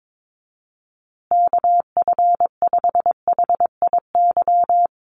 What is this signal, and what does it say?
Morse code, heard as a keyed tone, about 22 wpm, 700 Hz. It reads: KF5HIY
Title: USA Call Signs Repeated 3x 22wpm